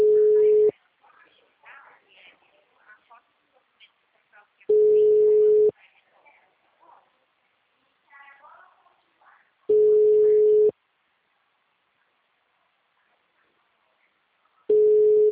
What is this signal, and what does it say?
Ringback tone (Europe, 425 Hz), with speech.
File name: out- -2004- - - .28